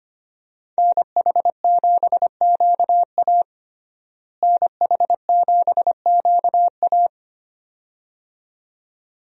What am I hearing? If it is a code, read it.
Morse code, heard as a keyed tone, about 25 wpm, 700 Hz. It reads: NH7QA NH7QA
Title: USA Call Signs - 5 Characters - Repeated 2x - 25wpm